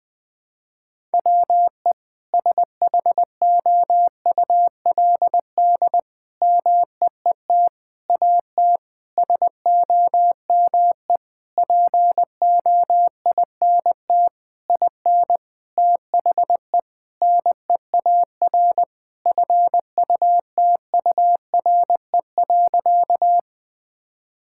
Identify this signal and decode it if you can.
Morse code, heard as a keyed tone, about 20 wpm, 700 Hz. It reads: WE SHOULD MEET AT SOME POINT IN THE NEAR FUTURE.
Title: Sentences with 7-Word Phrases - Spoken Beginning - 20wpm